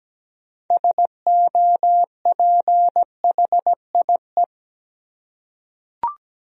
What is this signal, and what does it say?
Morse code, heard as a keyed tone, about 17 wpm, 700 Hz. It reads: SOPHIE
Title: USA Common Names - 17wpm